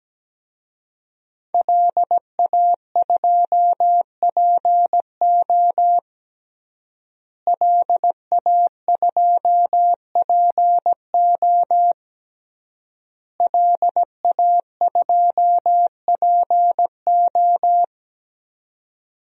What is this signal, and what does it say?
Morse code, heard as a keyed tone, about 17 wpm, 700 Hz. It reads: LA2PO LA2PO LA2PO
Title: DXCC-100 Call Signs - 5 Characters - Repeated 3x - 17wpm